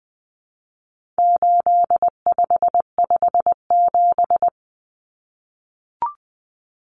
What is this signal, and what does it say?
Morse code, heard as a keyed tone, about 20 wpm, 700 Hz. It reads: 8557
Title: Numbers - 4-Digit 20wpm